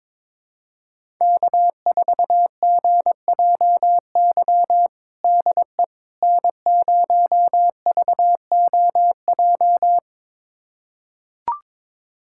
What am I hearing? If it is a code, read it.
Morse code, heard as a keyed tone, about 21 wpm, 700 Hz. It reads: K4GJY DE N0VOJ
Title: USA Callsign DE Callsign - 5 Characters - 22wpm